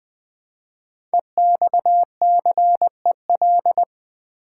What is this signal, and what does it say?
Morse code, heard as a keyed tone, about 20 wpm, 700 Hz. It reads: EXCEL
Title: Build A Word - 5 Letters At A Time - 10-Letter Words - 20wpm